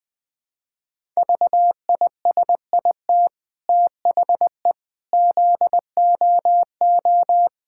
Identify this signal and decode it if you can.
Morse code, heard as a keyed tone, about 21 wpm, 700 Hz. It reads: VISIT THE ZOO
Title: Easy Sentences with Hints - 20wpm